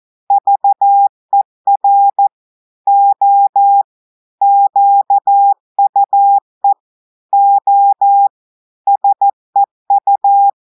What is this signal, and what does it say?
Morse code, heard as a keyed tone, about 14 wpm, 800 Hz. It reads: VER O QUE O SEU
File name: audios / portugais / portugais texte en clair 2 vitesse 14 mots minutes